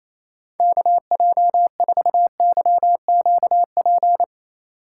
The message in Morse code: KJ4YQP

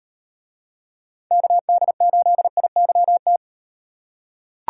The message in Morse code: KD8IYT

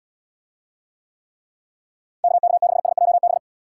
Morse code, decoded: KK6SJB

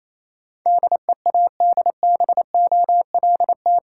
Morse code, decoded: DEADBOLT

